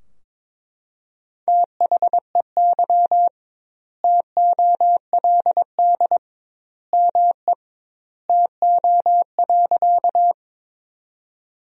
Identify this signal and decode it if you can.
Morse code, heard as a keyed tone, about 22 wpm, 700 Hz. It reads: THEY TOLD ME TO.
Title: 2x Word-Spacing - Sentences from Top 400 Words - 22wpm